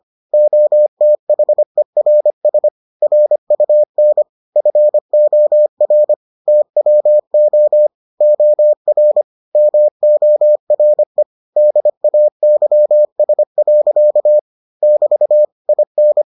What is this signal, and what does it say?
Morse code, heard as a keyed tone, about 25 wpm, 600 Hz. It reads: OTHERS RUN FOR TWO OR MORE DAYS. <BT> IN